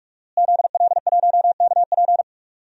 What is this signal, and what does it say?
Morse code, heard as a keyed tone, about 46 wpm, 700 Hz. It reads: ZL1KP
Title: DXCC-100 Call Signs - 5 Characters - Repeated 3x - 45wpm